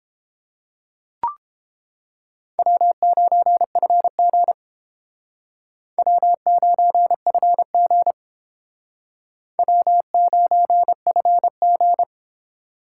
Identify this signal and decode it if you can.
Morse code, heard as a keyed tone, about 31 wpm, 700 Hz. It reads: W9FG W9FG W9FG